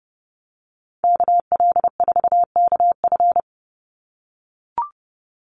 Morse code, heard as a keyed tone, about 30 wpm, 700 Hz. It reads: KL4KF
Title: USA Call Signs 30wpm